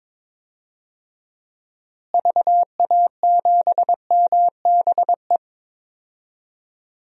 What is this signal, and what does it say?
Morse code, heard as a keyed tone, about 22 wpm, 700 Hz. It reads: VA7MBE
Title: DXCC-100 Call Signs - 6 Characters - Repeated 2x - 22wpm